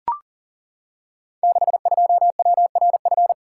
Morse code, heard as a keyed tone, about 39 wpm, 690 Hz. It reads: 62WRF
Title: Code Groups - 5 Letter:Numbers - 40wpm